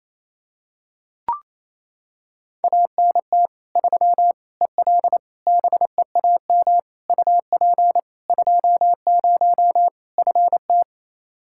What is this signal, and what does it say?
Morse code, heard as a keyed tone, about 27 wpm, 700 Hz. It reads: ANT 3 EL BEAM UP 20 FT
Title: Practice QSO - Exchange 2 - Info - 28wpm